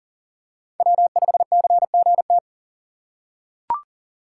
Morse code, heard as a keyed tone, about 40 wpm, 700 Hz. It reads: W5CGT
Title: USA Call Signs - 2x Character Spacing - 40wpm